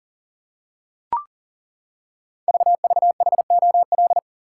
Morse code, heard as a keyed tone, about 38 wpm, 690 Hz. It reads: VVHOL
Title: Code Groups - 5 Letters - 40wpm